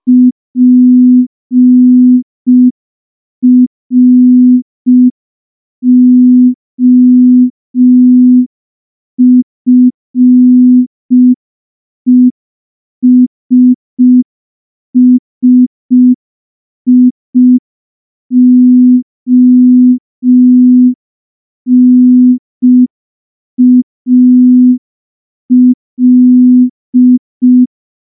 Morse code, heard as a keyed tone, about 5 wpm, 250 Hz. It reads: PROFESSIONAL
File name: morse / morse-training-05WPM